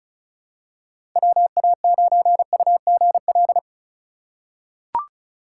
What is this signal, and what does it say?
Morse code, heard as a keyed tone, about 36 wpm, 700 Hz. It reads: WA9UGL
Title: USA Call Signs - 6 Characters - 2x Character Spacing - 35wpm